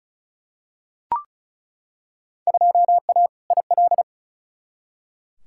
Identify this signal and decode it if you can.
Morse code, heard as a keyed tone, about 36 wpm, 700 Hz. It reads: 2A IL
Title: Field Day Exchanges - 35wpm